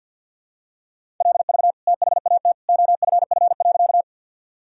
Morse code, heard as a keyed tone, about 52 wpm, 700 Hz. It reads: LV THAT OFF.
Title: Sentences from Top 400 Words and Abbreviations 50wpm